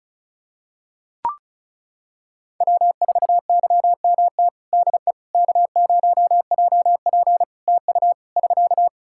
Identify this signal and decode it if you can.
Morse code, heard as a keyed tone, about 34 wpm, 700 Hz. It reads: W4YMT DE K0JP TU <SK>
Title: Practice QSO - Exchange 4 - Ending - 35wpm